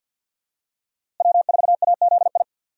Morse code, heard as a keyed tone, about 48 wpm, 700 Hz. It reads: W4AZI
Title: USA Call Signs - 5 Characters - Repeated 3x - 50wpm